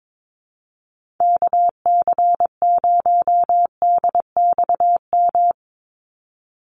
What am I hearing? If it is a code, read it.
Morse code, heard as a keyed tone, about 22 wpm, 700 Hz. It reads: KC0DXM
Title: USA Call Signs 22wpm